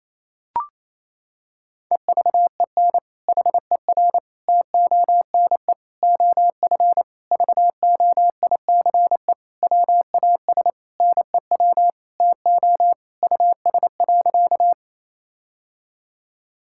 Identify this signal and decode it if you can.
Morse code, heard as a keyed tone, about 28 wpm, 700 Hz. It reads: EVEN HER TONE OF VOICE WAS NEW TO US.
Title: Sentences from Top 1600 Words 28wpm